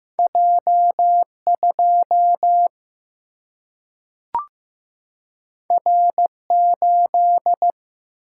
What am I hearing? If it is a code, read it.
Morse code, heard as a keyed tone, about 15 wpm, 700 Hz. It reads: J 2 R 8
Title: Single-Letter Single-Number 15wpm